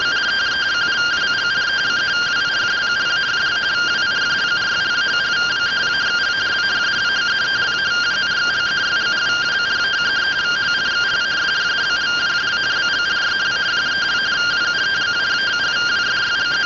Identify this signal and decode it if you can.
Two-tone teleprinter signal, about 170 Hz shift.